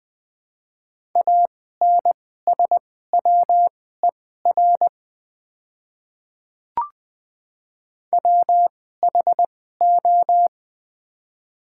Morse code, heard as a keyed tone, about 20 wpm, spaced as for 14 wpm, 700 Hz. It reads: ANSWER WHO
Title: 2x Character Spacing - Top 300 Words - 20wpm